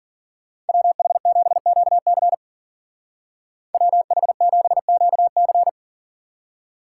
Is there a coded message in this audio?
Yes: Morse code from 0.7 to 5.7 s, heard as a keyed tone, about 50 wpm, 700 Hz. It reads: WH7QC WH7QC